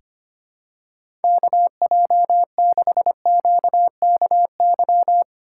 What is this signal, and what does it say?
Morse code, heard as a keyed tone, about 25 wpm, 700 Hz. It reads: KJ6QKY
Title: USA Call Signs - 6 Characters - 6x Character Spacing - 25wpm